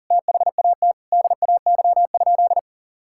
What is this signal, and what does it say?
Morse code, heard as a keyed tone, about 37 wpm, 700 Hz. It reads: THAT DAY?
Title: Sentences from Top 1600 Words 40wpm